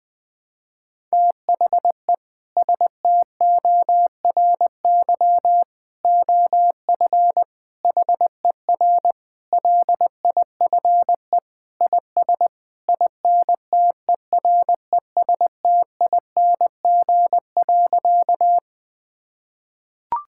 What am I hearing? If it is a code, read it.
Morse code, heard as a keyed tone, about 20 wpm, 700 Hz. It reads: THE STORY OF HER LIFE IS INTERESTING.
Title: Sentences with 3-Word Phrases - Spoken Beginning - 20wpm